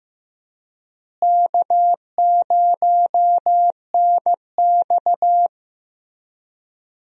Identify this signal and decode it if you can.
Morse code, heard as a keyed tone, about 16 wpm, 700 Hz. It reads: K0NX
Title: USA Call Signs - 4 Characters - Repeated 2x - 15wpm